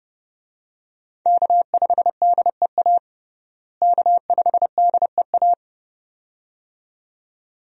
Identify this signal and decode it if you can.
Morse code, heard as a keyed tone, about 32 wpm, 700 Hz. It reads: K5DEA K5DEA
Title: USA Call Signs - 5 Characters - Repeated 2x - 30wpm